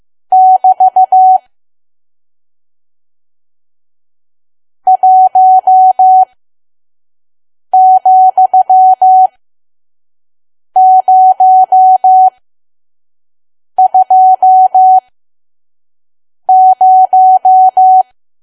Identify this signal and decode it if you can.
Morse code, heard as a keyed tone, about 15 wpm, spaced as for 5 wpm, 750 Hz. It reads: <BT> 1,020
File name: Archive / 5 WPM files / PM